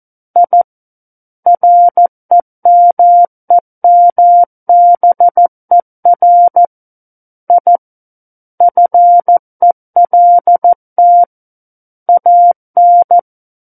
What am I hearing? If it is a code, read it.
Morse code, heard as a keyed tone, about 14 wpm, 700 Hz. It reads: I REMEMBER I FELT AN
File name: Book One / 05-Chapter 5 - 14 WPM